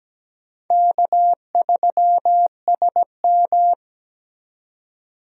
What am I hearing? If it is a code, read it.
Morse code, heard as a keyed tone, about 17 wpm, 700 Hz. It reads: K3SM
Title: USA Call Signs - 4 Characters - 17wpm